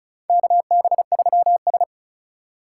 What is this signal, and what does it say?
Morse code, heard as a keyed tone, about 34 wpm, 700 Hz. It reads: KB3S
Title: USA Call Signs - 3x Character Spacing - 35wpm